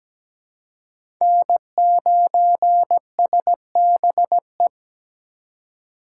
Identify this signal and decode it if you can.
Morse code, heard as a keyed tone, about 20 wpm, 700 Hz. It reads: N9SBE
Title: Speed Racing - USA 1x3 Call Signs 17wpm